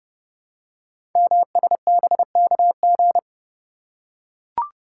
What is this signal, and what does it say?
Morse code, heard as a keyed tone, about 30 wpm, 700 Hz. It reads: MSBKG
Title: Code Groups - 5 Letters - 30wpm